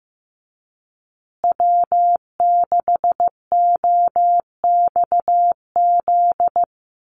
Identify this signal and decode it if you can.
Morse code, heard as a keyed tone, about 15 wpm, 700 Hz. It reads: W6OXZ
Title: USA Call Signs Repeated 2x 15wpm